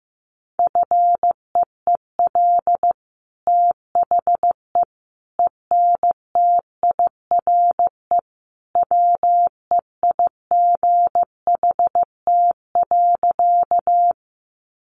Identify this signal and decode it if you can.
Morse code, heard as a keyed tone, about 15 wpm, 700 Hz. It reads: FEEL THE ENTIRE WEIGHT.
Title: Sentences from Top 800 Words 15wpm